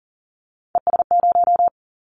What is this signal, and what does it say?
Morse code, heard as a keyed tone, about 40 wpm, 700 Hz. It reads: ES0
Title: Callsign Prefixes for Top 100 DXCC - 40wpm